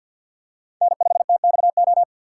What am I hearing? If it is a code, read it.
Morse code, heard as a keyed tone, about 50 wpm, 700 Hz. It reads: N5TXO
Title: USA Call Signs - 5 Characters - 3x Character Spacing - 50wpm